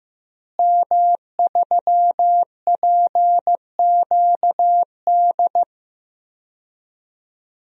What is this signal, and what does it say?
Morse code, heard as a keyed tone, about 15 wpm, 700 Hz. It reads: M3PQD
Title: DXCC-100 Call Signs - 5 Characters - Repeated 3x - 15wpm